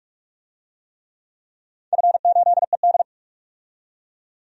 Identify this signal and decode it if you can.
Morse code, heard as a keyed tone, about 44 wpm, 700 Hz. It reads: F8ED
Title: DXCC-100 Call Signs - 4 Characters - 45wpm